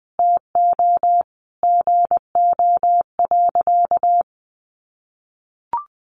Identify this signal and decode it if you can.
Morse code, heard as a keyed tone, about 19 wpm, 700 Hz. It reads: TO GO.
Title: Sentences from Top 1400 Words 20wpm